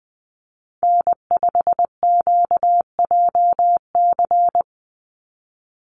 Morse code, heard as a keyed tone, about 20 wpm, 700 Hz. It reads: N5QJC
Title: USA Call Signs 20wpm